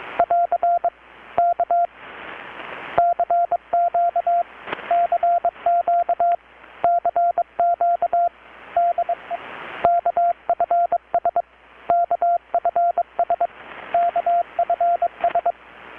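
Morse code, heard as a keyed tone, about 21 wpm, 690 Hz, with radio noise. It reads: <AR> K CQ CQ CQ DE KFS KFS KFS
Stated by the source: Morse from KPH's Night of Nights XX maritime radiotelegraphy event, received in Chiapas, Mexico about 3,800 km away on a Kenwood R-600 with a grounded 10 m long-wire antenna.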